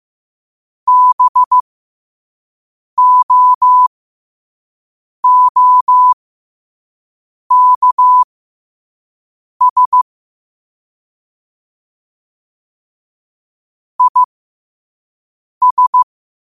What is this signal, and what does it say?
Morse code, heard as a keyed tone, about 15 wpm, spaced as for 5 wpm, 1000 Hz. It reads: BOOKS IS
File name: qotdfiles / 5wpm